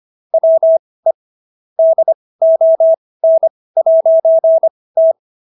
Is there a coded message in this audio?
Yes: Morse code from 0.3 to 5.1 s, heard as a keyed tone, about 25 wpm, spaced as for 18 wpm, 650 Hz. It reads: WE DON'T